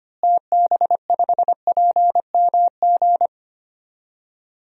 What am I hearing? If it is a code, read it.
Morse code, heard as a keyed tone, about 25 wpm, 700 Hz. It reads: TB5PMG
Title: DXCC-100 Call Signs - 6 Characters - Repeated 3x - 25wpm